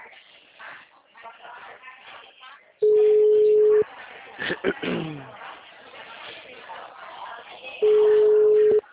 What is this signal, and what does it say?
Ringback tone (Europe, 425 Hz), with speech.